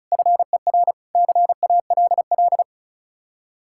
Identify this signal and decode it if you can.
Morse code, heard as a keyed tone, about 46 wpm, 700 Hz. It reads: FER CALL